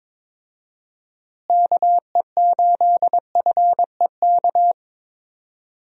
Morse code, heard as a keyed tone, about 22 wpm, 700 Hz. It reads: KE8FEK
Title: USA Call Signs - 6 Characters - Repeated 3x - 22wpm